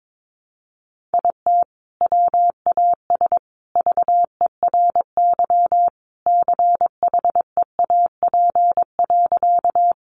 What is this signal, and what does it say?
Morse code, heard as a keyed tone, about 22 wpm, 700 Hz. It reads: IT WAS VERY CHEAP.
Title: Sentences from Top 1200 Words 22wpm